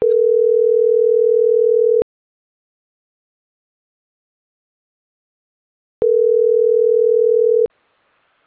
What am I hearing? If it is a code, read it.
Ringback tone (North America), 440 + 480 Hz.